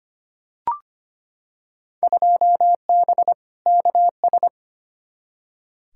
Morse code, heard as a keyed tone, about 25 wpm, 700 Hz. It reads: 2B KS
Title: Field Day Exchanges Encore - 25wpm